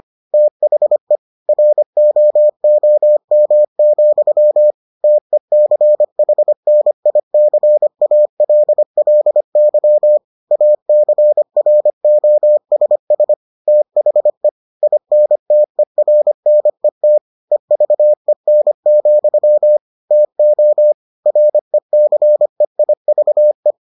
Morse code, heard as a keyed tone, about 25 wpm, 600 Hz. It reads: THE ROOM, TECHNICALLY ACROSS THE INTERNET EVEN, TO RECEIVE